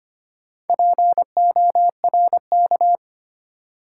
Morse code, heard as a keyed tone, about 25 wpm, 690 Hz. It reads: PORK
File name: practice / 25wpm / Easy Words with Hints - 25wpm